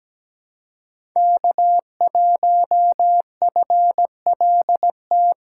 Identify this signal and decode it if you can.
Morse code, heard as a keyed tone, about 17 wpm, 700 Hz. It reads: K1FLT
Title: USA Call Signs - 5 Characters - 3x Character Spacing - 17wpm